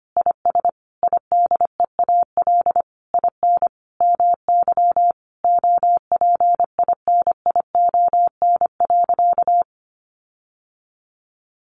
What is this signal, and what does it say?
Morse code, heard as a keyed tone, about 25 wpm, 700 Hz. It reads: IS IDEAL IN MY OPINION.
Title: Sentences from Top 1400 Words 25wpm